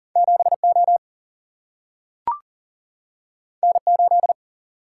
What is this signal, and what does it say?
Morse code, heard as a keyed tone, about 40 wpm, 700 Hz. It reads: 7O N8
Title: Single-Letter Single-Number 40wpm